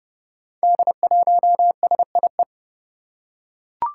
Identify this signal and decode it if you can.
Morse code, heard as a keyed tone, about 30 wpm, 700 Hz. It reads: D1SIE
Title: Five Challenging Letter-Number Combinations 30wpm